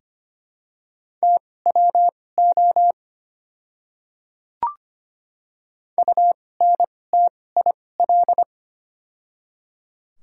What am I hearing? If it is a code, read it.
Morse code, heard as a keyed tone, about 25 wpm, spaced as for 18 wpm, 700 Hz. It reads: TWO UNTIL
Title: 2x Character Spacing - Top 500 Words - 25wpm